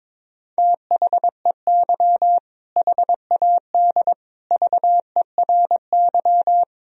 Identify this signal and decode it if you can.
Morse code, heard as a keyed tone, about 22 wpm, 700 Hz. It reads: THEY HAD VERY
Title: Sentences from Top 1000 Words - Review Part 2 - 22wpm